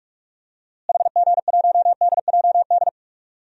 Morse code, heard as a keyed tone, about 46 wpm, 700 Hz. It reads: HG1DJD
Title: DXCC-100 Call Signs - 6 Characters - Repeated 3x - 45wpm